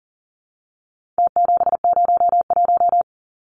Morse code, heard as a keyed tone, about 40 wpm, 700 Hz. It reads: T701
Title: Numbers - Contest Style - 4-Digit 40wpm